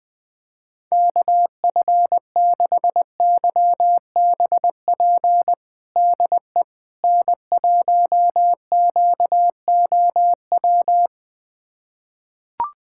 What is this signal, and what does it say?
Morse code, heard as a keyed tone, about 20 wpm, 700 Hz. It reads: KF6YBP DE N1QOW
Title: USA Callsign DE Callsign 20wpm